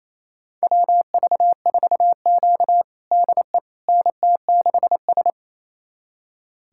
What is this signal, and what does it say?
Morse code, heard as a keyed tone, about 28 wpm, 700 Hz. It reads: WV4Q DE NT6S
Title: USA Callsign DE Callsign - 4 Characters - 28wpm